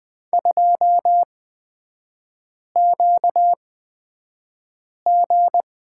Morse code, heard as a keyed tone, about 20 wpm, spaced as for 5 wpm, 690 Hz. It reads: 2QG
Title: 236 Lesson 38e - Letter q - Three Characters 20wpm